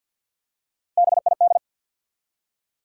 Morse code, heard as a keyed tone, about 49 wpm, 700 Hz. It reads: BID
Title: Three Letter Words - 50wpm